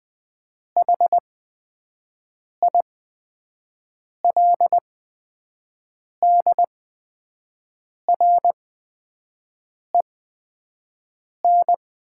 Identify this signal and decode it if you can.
Morse code, heard as a keyed tone, about 20 wpm, spaced as for 5 wpm, 700 Hz. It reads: HILDREN
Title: 8x Character Spacing - Top 500 Words - 20wpm